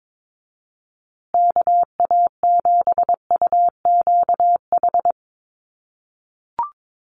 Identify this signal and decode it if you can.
Morse code, heard as a keyed tone, about 22 wpm, 700 Hz. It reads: KA7UQH
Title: USA Call Signs Repeated 2x 22wpm